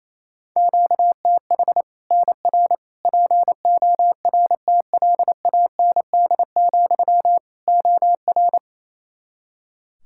Morse code, heard as a keyed tone, about 28 wpm, 700 Hz. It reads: QTH NR PORTLAND, OR
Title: Practice QSO - Exchange 1 - Beginning - 28wpm